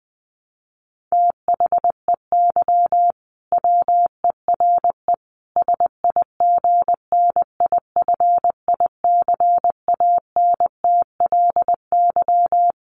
Morse code, heard as a keyed tone, about 20 wpm, 700 Hz. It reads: THEY WERE SIGNIFICANTLY
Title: Sentences from Top 1500 Words 20wpm